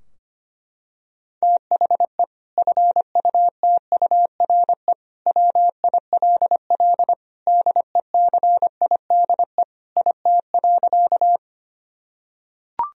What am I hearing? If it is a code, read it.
Morse code, heard as a keyed tone, about 25 wpm, 700 Hz. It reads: THE FUTURE WILL DECIDE IT.
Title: Sentences from Top 1000 Words - Review Part 3 - 25wpm